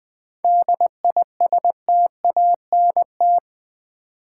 Morse code, heard as a keyed tone, about 20 wpm, 700 Hz. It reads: DISTANT
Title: 7 Letter Words - First 3 Letters Given - 20wpm